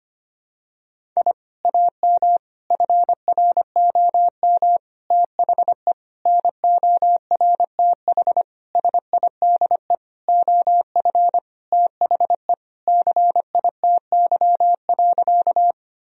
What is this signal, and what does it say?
Morse code, heard as a keyed tone, about 25 wpm, 700 Hz. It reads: I AM FROM THE NORTH SIDE OF THE CITY.
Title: Sentences from Top 1000 Words - Review Part 1 - 25wpm